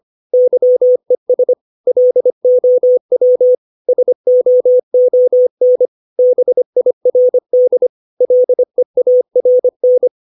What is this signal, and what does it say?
Morse code, heard as a keyed tone, about 25 wpm, 500 Hz. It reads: YES LOW SOON BIRD LEARN
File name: 632 words / 632 Words 25WPM